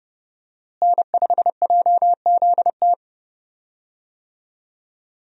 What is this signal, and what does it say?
Morse code, heard as a keyed tone, about 29 wpm, 700 Hz. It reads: N5JZT
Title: USA Call Signs - 5 Characters - Repeated 2x - 30wpm